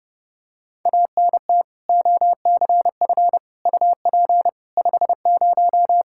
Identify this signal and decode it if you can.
Morse code, heard as a keyed tone, about 32 wpm, 700 Hz. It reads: ANT OCF UP 50
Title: Practice QSO - Exchange 2 - Info - 30wpm